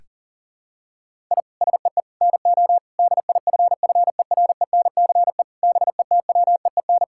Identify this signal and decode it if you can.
Morse code, heard as a keyed tone, about 40 wpm, 700 Hz. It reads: I SEE NO DIFFERENCE BETWEEN
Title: Sentences from Top 1000 Words - Review Part 5 - 40wpm